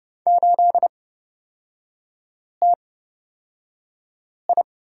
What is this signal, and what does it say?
Morse code, heard as a keyed tone, about 31 wpm, spaced as for 5 wpm, 700 Hz. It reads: 8TI
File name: Lesson 36e - BT - Three Characters 30wpm